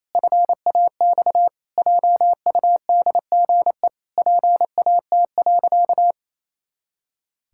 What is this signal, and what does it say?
Morse code, heard as a keyed tone, about 28 wpm, 700 Hz. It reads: FAX JUDGE PAT.
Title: Pangram Challenge v2 - 28wpm